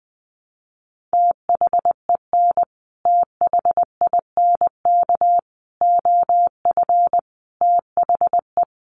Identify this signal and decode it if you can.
Morse code, heard as a keyed tone, about 20 wpm, 700 Hz. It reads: THEN THINK OF THE
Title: Sentences from Top 1000 Words 20wpm